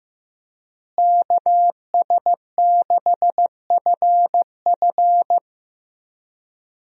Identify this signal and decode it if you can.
Morse code, heard as a keyed tone, about 15 wpm, 700 Hz. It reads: KS6FF
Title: USA Call Signs - 5 Characters - Repeated 2x - 15wpm